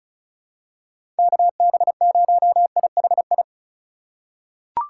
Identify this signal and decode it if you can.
Morse code, heard as a keyed tone, about 36 wpm, 700 Hz. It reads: KB0IHI
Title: USA Call Signs - 6 Characters - 3x Character Spacing - 35wpm